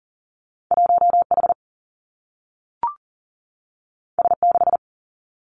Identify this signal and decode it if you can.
Morse code, heard as a keyed tone, about 39 wpm, 700 Hz. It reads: 1 H S 6
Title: Two Challenging Letter-Number Combinations 40wpm